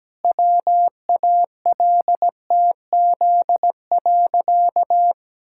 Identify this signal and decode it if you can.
Morse code, heard as a keyed tone, about 17 wpm, 700 Hz. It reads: WALTZ.
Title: Pangram Challenge v2 - 17wpm